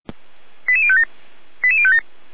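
Five-tone selective call (ZVEI), heard as quick runs of five tones, tones of about 70 ms. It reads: 88057 88057
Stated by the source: ZVEI five-tone selective call 88367, received by SDR.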